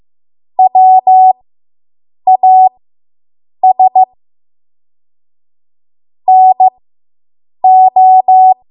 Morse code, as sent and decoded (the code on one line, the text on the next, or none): .-- .- ... / -. ---
WAS NO